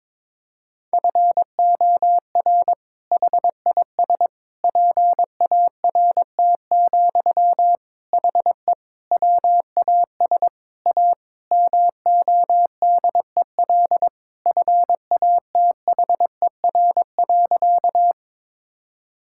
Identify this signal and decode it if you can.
Morse code, heard as a keyed tone, about 21 wpm, 700 Hz. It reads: FOR HIS PART, HE WAS A MODEL FATHER.
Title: Sentences from Top 1000 Words - Review Part 5 - 22wpm